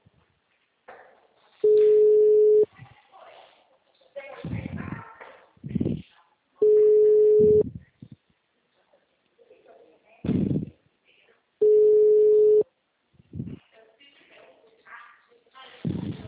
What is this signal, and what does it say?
Ringback tone (Europe, 425 Hz), with speech.
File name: out- -2021- - - .7